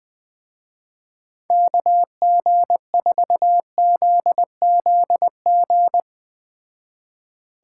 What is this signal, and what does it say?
Morse code, heard as a keyed tone, about 20 wpm, 700 Hz. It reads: KG4ZZG